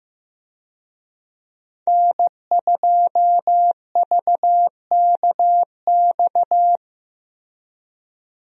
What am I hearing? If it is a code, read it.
Morse code, heard as a keyed tone, about 16 wpm, 700 Hz. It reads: N2VKX
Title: USA Call Signs - 5 Characters - Repeated 2x - 15wpm